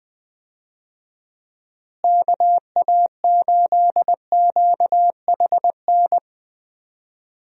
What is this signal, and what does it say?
Morse code, heard as a keyed tone, about 21 wpm, 700 Hz. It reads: KA8QHN
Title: USA Call Signs - 6 Characters - Repeated 3x - 20wpm